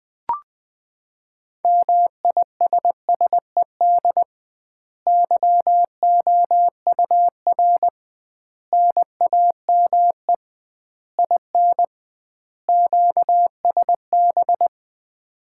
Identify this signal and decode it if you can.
Morse code, heard as a keyed tone, about 20 wpm, 700 Hz. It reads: MISSED YOUR NAME IN QSB
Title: Common QSO Phrases - 2x Word-Spacing - 20wpm